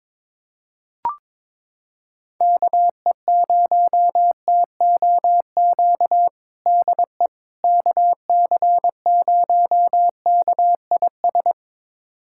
Morse code, heard as a keyed tone, about 22 wpm, 700 Hz. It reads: KE0TOQ DE KC0KIS
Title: USA Callsign DE Callsign - 6 Characters - 22wpm